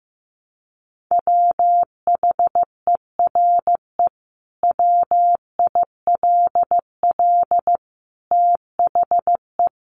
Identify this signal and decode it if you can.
Morse code, heard as a keyed tone, about 15 wpm, 700 Hz. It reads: WHERE WILL THE
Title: Sentences from Top 1500 Words 15wpm